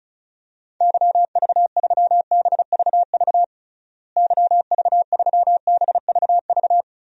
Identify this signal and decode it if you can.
Morse code, heard as a keyed tone, about 36 wpm, 700 Hz. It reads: YV3BVV YV3BVV
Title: DXCC-100 Call Signs - 6 Characters - Repeated 3x - 35wpm